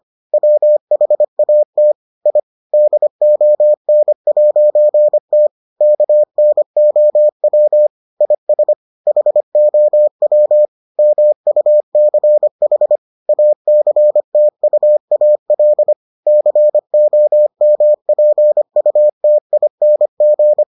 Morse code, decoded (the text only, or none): WHAT I DON'T KNOW IS HOW MUCH ACTUAL COMPUTING